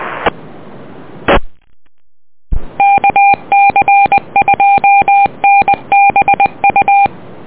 Morse code, heard as a keyed tone, about 20 wpm, 800 Hz. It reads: E E KC2NBU
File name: net audio